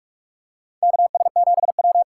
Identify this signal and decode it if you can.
Morse code, heard as a keyed tone, about 44 wpm, 700 Hz. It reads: KS7W
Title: USA Call Signs - 4 Characters - 6x Character Spacing - 45wpm